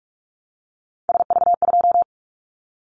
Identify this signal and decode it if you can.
Morse code, heard as a keyed tone, about 46 wpm, 700 Hz. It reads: SV2
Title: Callsign Prefixes for Top 50 DXCC - 45wpm